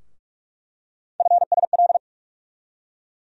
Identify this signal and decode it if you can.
Morse code, heard as a keyed tone, about 44 wpm, 700 Hz. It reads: FSL
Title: USA Call Sign Postfixes - 3 Letters 45wpm